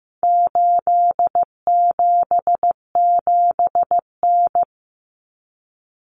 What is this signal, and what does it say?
Morse code, heard as a keyed tone, about 15 wpm, 700 Hz. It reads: 877N
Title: Common Cut Numbers - 4-Digit 15wpm